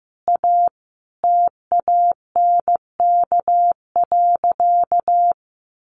Morse code, heard as a keyed tone, about 15 wpm, 700 Hz. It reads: A TANK.
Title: Easy Phrases 15wpm